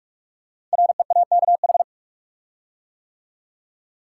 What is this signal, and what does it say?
Morse code, heard as a keyed tone, about 44 wpm, 690 Hz. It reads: REAKH
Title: Code Groups - 5 Letter:Numbers - 45wpm